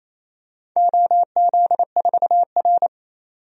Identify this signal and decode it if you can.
Morse code, heard as a keyed tone, about 27 wpm, 700 Hz. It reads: OZ4R
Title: DXCC-100 Call Signs - 4 Characters - 28wpm